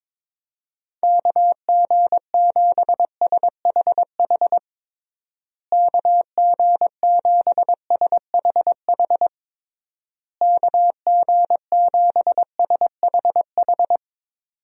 Morse code, heard as a keyed tone, about 22 wpm, 700 Hz. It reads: KG7SHH KG7SHH KG7SHH
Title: USA Call Signs - 6 Characters - Repeated 3x - 22wpm